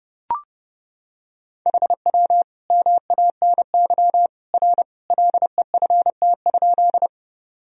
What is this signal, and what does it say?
Morse code, heard as a keyed tone, about 30 wpm, 700 Hz. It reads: HW MANY R LEFT?
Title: Sentences from Top 300 Words and Abbreviations 30wpm